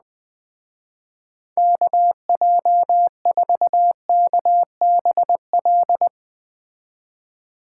Morse code, heard as a keyed tone, about 20 wpm, 700 Hz. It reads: KJ4KBL